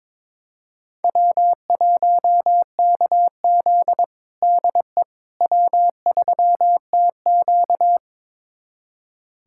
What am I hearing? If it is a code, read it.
Morse code, heard as a keyed tone, about 22 wpm, 700 Hz. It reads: W1KZ DE W3TQ
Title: USA Callsign DE Callsign - 4 Characters - 22wpm